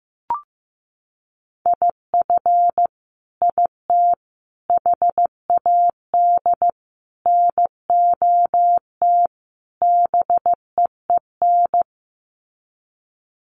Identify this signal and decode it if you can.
Morse code, heard as a keyed tone, about 15 wpm, 700 Hz. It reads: IF IT HAD NOT BEEN
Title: Sets of 5 Words Encore 15wpm